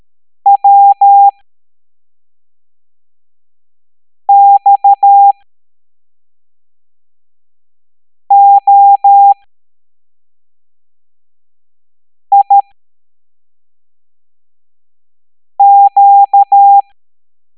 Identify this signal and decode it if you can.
Morse code, heard as a keyed tone, about 13 wpm, spaced as for 3 wpm, 800 Hz. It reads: WXOIQ
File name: hamdata / L09-EX04